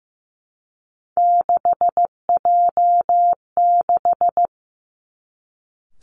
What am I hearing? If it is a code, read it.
Morse code, heard as a keyed tone, about 15 wpm, 700 Hz. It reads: 6J6
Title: Three Challenging Letter-Number Combinations 15wpm